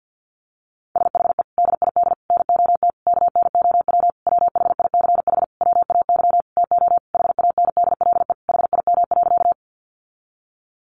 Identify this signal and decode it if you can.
Morse code, heard as a keyed tone, about 50 wpm, 700 Hz. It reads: SHE DID NOT KNOW WHICH WAY TO HANDLE HIM.
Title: Sentences from Top 1100 Words 50wpm